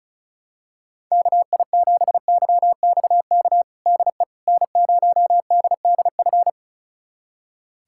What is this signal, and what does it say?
Morse code, heard as a keyed tone, about 34 wpm, 690 Hz. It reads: KI7YXK DE N0DDF
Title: USA Callsign DE Callsign 35wpm